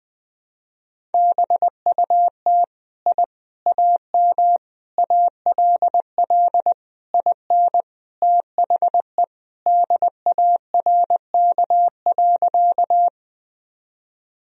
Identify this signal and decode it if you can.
Morse code, heard as a keyed tone, about 20 wpm, 700 Hz. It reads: BUT I AM ALL IN THE DARK.
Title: Sentences from Top 1000 Words - Review Part 2 - 20wpm